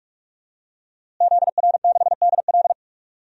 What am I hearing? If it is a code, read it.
Morse code, heard as a keyed tone, about 46 wpm, 700 Hz. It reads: ZR6DL
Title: DXCC-100 Call Signs - 5 Characters - Repeated 3x - 45wpm